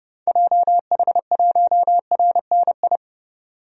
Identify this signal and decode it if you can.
Morse code, heard as a keyed tone, about 30 wpm, 700 Hz. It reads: JH1RNI
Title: DXCC-100 Call Signs - 6 Characters - 30wpm